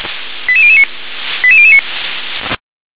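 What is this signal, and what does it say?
Five-tone selective call (ZVEI), heard as quick runs of five tones, tones of about 70 ms. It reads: 88009 88009
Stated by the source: Five-tone ZVEI selective call received by SDR.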